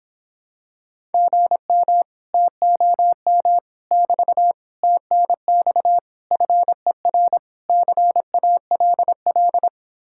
Morse code, heard as a keyed tone, about 26 wpm, 700 Hz. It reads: GM TOM <BT> TNX FER CALL